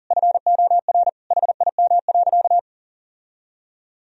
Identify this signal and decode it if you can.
Morse code, heard as a keyed tone, about 40 wpm, 700 Hz. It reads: FOR HIM.
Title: Sentences from Top 1000 Words - Review Part 4 - 40wpm